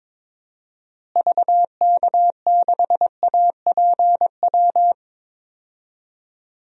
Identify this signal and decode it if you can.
Morse code, heard as a keyed tone, about 22 wpm, 700 Hz. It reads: VK6APW
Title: DXCC-100 Call Signs - 6 Characters - Repeated 3x - 22wpm